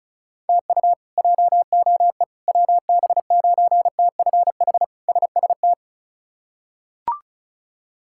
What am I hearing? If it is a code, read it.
Morse code, heard as a keyed tone, about 36 wpm, 700 Hz. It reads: TU JOE WB9TFH SST
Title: K1USN SST QSO Practice - 35wpm